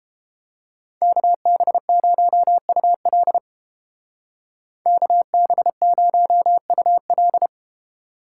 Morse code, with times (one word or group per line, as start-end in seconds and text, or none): 1.0-3.4 KB0UL
4.9-7.5 KB0UL